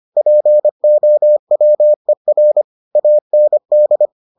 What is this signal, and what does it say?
Morse code, heard as a keyed tone, about 25 wpm, 600 Hz. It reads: POWER AND